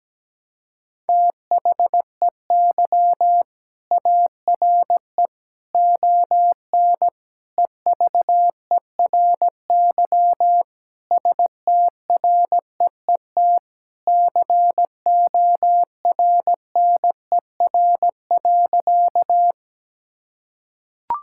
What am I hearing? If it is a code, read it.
Morse code, heard as a keyed tone, about 17 wpm, 700 Hz. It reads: THEY ARE ON EVERY STREET CORNER.
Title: Sentences from Top 1600 Words 17wpm